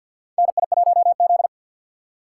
Morse code, heard as a keyed tone, about 50 wpm, 700 Hz. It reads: NI1Z
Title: USA Call Signs - 4 Characters - 3x Character Spacing - 50wpm